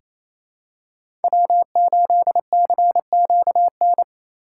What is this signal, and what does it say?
Morse code, heard as a keyed tone, about 28 wpm, 700 Hz. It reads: W8CQN
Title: USA Call Signs - 5 Characters - 2x Character Spacing - 28wpm